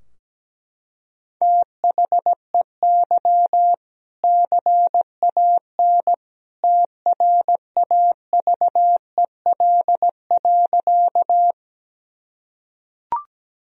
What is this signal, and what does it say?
Morse code, heard as a keyed tone, about 17 wpm, 700 Hz. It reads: THEY CAN TRAVEL.
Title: Sentences from Top 1000 Words - Review Part 2 - 17wpm